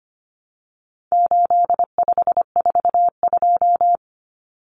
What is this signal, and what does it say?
Morse code, heard as a keyed tone, about 25 wpm, 700 Hz. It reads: 8542
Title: Common Cut Numbers - 4-Digit 25wpm